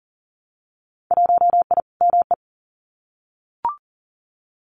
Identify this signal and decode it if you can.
Morse code, heard as a keyed tone, about 40 wpm, 700 Hz. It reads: 1I ME
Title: Winter Field Day Exchanges 40wpm